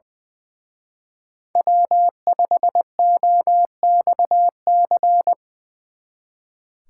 Morse code, heard as a keyed tone, about 24 wpm, 700 Hz. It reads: W5OXC
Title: Speed Racing - USA 1x3 Call Signs 20wpm